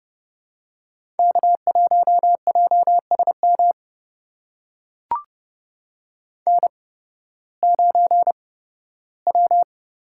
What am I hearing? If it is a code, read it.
Morse code, heard as a keyed tone, about 30 wpm, spaced as for 16 wpm, 700 Hz. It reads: K1JSM N9W
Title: USA Call Signs - 5 Characters - 8x Character Spacing - 30wpm